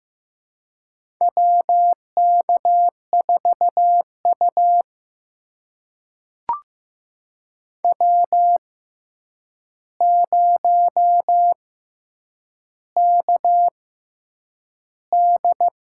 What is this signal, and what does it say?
Morse code, heard as a keyed tone, about 15 wpm, spaced as for 8 wpm, 700 Hz. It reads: WK4U W0KD
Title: USA Call Signs - 4 Characters - 6x Character Spacing - 15wpm